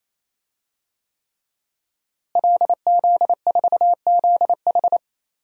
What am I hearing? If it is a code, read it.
Morse code, heard as a keyed tone, about 27 wpm, 700 Hz. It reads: LZ4ZH
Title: DXCC-100 Call Signs - 5 Characters - Repeated 3x - 28wpm